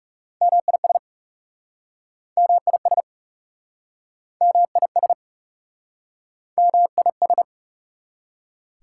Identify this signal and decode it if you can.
Morse code, heard as a keyed tone, about 38 wpm, 690 Hz. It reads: MIS MIS MIS MIS